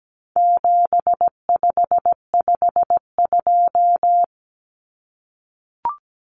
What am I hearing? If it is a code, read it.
Morse code, heard as a keyed tone, about 17 wpm, 700 Hz. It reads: 7552
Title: Common Cut Numbers - 4-Digit 17wpm